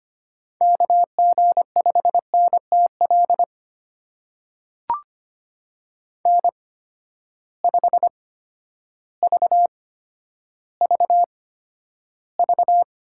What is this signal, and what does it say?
Morse code, heard as a keyed tone, about 25 wpm, spaced as for 11 wpm, 700 Hz. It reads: KG5NTL N5VVV
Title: USA Call Signs - 8x Character Spacing - 25wpm